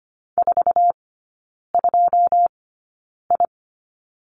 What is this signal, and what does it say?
Morse code, heard as a keyed tone, about 25 wpm, spaced as for 9 wpm, 700 Hz. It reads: 42I
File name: Lesson 14e - Number 2 - Three Characters 25wpm